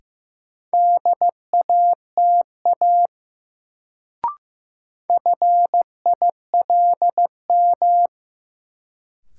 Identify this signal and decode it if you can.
Morse code, heard as a keyed tone, about 15 wpm, 700 Hz. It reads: DATA FILM
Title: Sentences from Top 1000 Words - Review Part 3 - New Word Drill - 15wpm